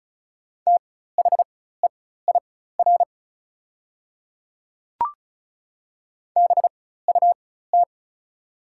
Morse code, heard as a keyed tone, about 34 wpm, spaced as for 16 wpm, 690 Hz. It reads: THEIR BUT
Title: 4x Character Spacing - Top 100 Words - 35wpm